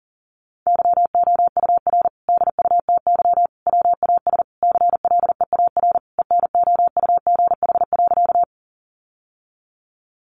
Morse code, heard as a keyed tone, about 40 wpm, 700 Hz. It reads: YOUR DUTY WAS CLEAR ENOUGH.
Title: Sentences from Top 1500 Words 40wpm